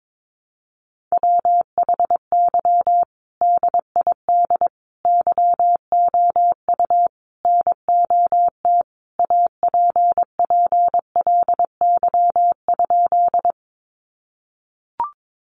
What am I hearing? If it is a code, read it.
Morse code, heard as a keyed tone, about 22 wpm, 700 Hz. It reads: WHY DID YOU NOT APPLY?
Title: Sentences from Top 1200 Words 22wpm